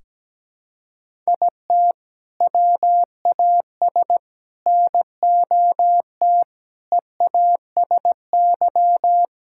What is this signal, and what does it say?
Morse code, heard as a keyed tone, about 17 wpm, 700 Hz. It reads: IT WAS NOT EASY
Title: Sentences from Top 1000 Words - Review Part 6 - 17wpm